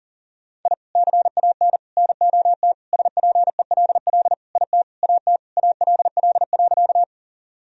Morse code, heard as a keyed tone, about 40 wpm, 700 Hz. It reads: I CAN NOT SPELL IT AT ALL.
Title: Sentences from Top 1000 Words - Review Part 1 - 40wpm